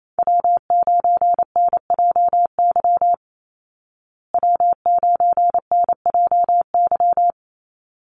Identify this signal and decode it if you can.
Morse code, heard as a keyed tone, about 28 wpm, 700 Hz. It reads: W9NJY W9NJY
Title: Call Signs Repeated 3x - 28wpm